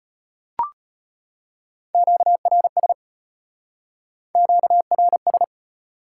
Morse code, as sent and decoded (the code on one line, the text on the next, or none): --.- .-. ... / --.- .-. ...
QRS QRS